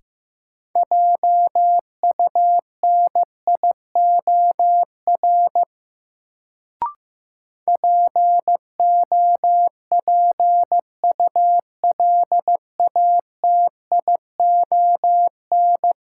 Morse code, heard as a keyed tone, about 15 wpm, 700 Hz. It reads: JUNIOR POPULATION
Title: Sentences from Top 1000 Words - Review Part 7 - New Word Drill - 15wpm